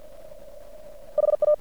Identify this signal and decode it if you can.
Morse code, heard as a keyed tone, about 50 wpm, 600 Hz. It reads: HA
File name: archived / CRSnet